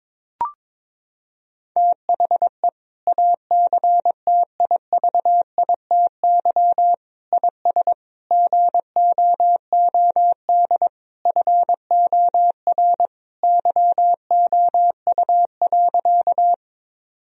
Morse code, heard as a keyed tone, about 22 wpm, 700 Hz. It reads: THE ACTIVITY IS GOOD FOR YOU.
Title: Sentences from Top 1000 Words - Review Part 4 - 22wpm